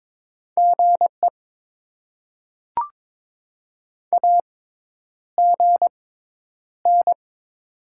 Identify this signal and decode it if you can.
Morse code, heard as a keyed tone, about 22 wpm, spaced as for 10 wpm, 700 Hz. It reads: GE AGN
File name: Top 20 QSO Elements - 6x Character Spacing - 22wpm